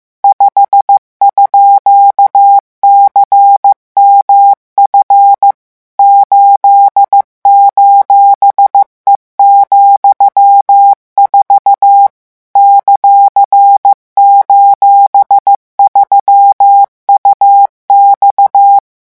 Morse code, heard as a keyed tone, about 15 wpm, 800 Hz. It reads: 5_CMF 8:E,4 ;:3UX